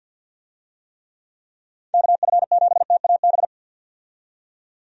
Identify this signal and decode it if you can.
Morse code, heard as a keyed tone, about 50 wpm, 700 Hz. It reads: KF7TAB
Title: USA Call Signs - 6 Characters - 50wpm